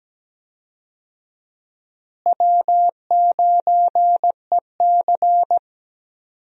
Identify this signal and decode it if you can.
Morse code, heard as a keyed tone, about 17 wpm, 700 Hz. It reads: W9EC
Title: USA Call Signs - 4 Characters - Repeated 2x - 17wpm